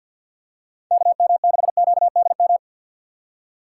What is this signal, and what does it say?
Morse code, heard as a keyed tone, about 50 wpm, 700 Hz. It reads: KM6QDM
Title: USA Call Signs - 6 Characters - 3x Character Spacing - 50wpm